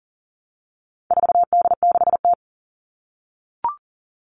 Morse code, heard as a keyed tone, about 40 wpm, 700 Hz. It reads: 4D6T
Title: All Cut Numbers - 4-Digit 40wpm